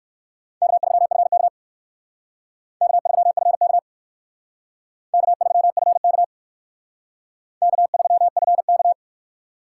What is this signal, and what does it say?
Morse code, heard as a keyed tone, about 56 wpm, 700 Hz. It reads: K3FK K3FK K3FK K3FK
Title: Speed Racing - USA 1x2 Call Signs 45wpm